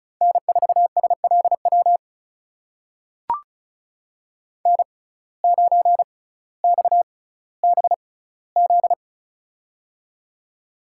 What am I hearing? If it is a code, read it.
Morse code, heard as a keyed tone, about 36 wpm, spaced as for 18 wpm, 700 Hz. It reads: N4SLW N9XBZ
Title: USA Call Signs - 6x Character Spacing - 35wpm